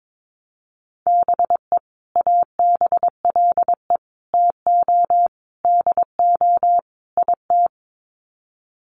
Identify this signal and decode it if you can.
Morse code, heard as a keyed tone, about 22 wpm, 700 Hz. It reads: BE ABLE TO DO IT
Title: Sets of 5 Words Encore2 22wpm